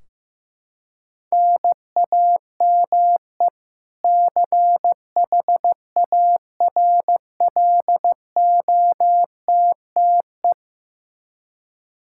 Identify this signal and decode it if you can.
Morse code, heard as a keyed tone, about 15 wpm, 700 Hz. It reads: NAME CHARLOTTE
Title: Practice QSO - Exchange 1 - Beginning - 15wpm